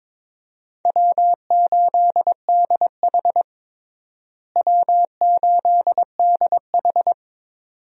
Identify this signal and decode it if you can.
Morse code, heard as a keyed tone, about 22 wpm, 700 Hz. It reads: W8DH W8DH
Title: USA Call Signs - 4 Characters - Repeated 2x - 22wpm